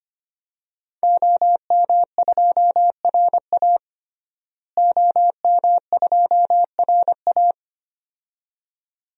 Morse code, heard as a keyed tone, about 25 wpm, 700 Hz. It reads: OM2RA OM2RA
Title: DXCC-100 Call Signs - 5 Characters - Repeated 2x - 25wpm